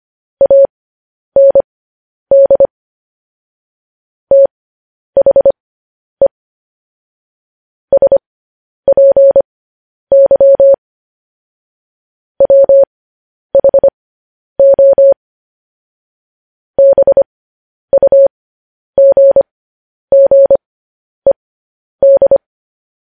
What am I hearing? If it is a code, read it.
Morse code, heard as a keyed tone, about 25 wpm, spaced as for 10 wpm, 550 Hz. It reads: AND THE SPY WHO BUGGED
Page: News Headlines in Morse Code at 10 WPM